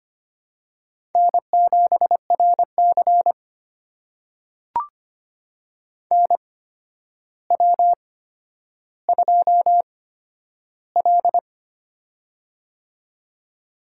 Morse code, heard as a keyed tone, about 25 wpm, spaced as for 11 wpm, 700 Hz. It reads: N7RC NW2L
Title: USA Call Signs - 4 Characters - 8x Character Spacing - 25wpm